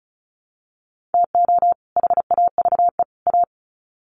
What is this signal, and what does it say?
Morse code, heard as a keyed tone, about 34 wpm, 700 Hz. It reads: TO HAVE A
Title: Sets of 3 Words 35wpm